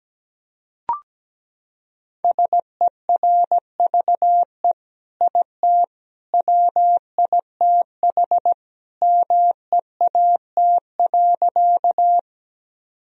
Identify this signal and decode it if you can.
Morse code, heard as a keyed tone, about 17 wpm, 700 Hz. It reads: SERVE IT WITH MEAT.
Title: Sentences from Top 1000 Words - Review Part 3 - 17wpm